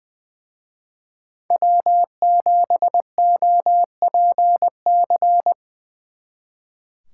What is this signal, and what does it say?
Morse code, heard as a keyed tone, about 24 wpm, 700 Hz. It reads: W7OPC
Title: Speed Racing - USA 1x3 Call Signs 20wpm